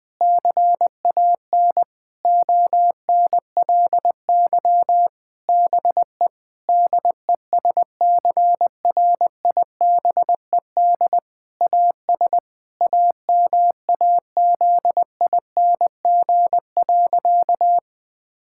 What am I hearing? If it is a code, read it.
Morse code, heard as a keyed tone, about 21 wpm, 700 Hz. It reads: CAN ONLY BE DESCRIBED AS AMAZING.
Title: Sentences with 6-Word Phrases - 20wpm